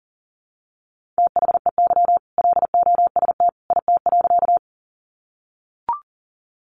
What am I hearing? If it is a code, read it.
Morse code, heard as a keyed tone, about 40 wpm, 700 Hz. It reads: THEY LOST IT.
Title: Sentences from Top 1400 Words 40wpm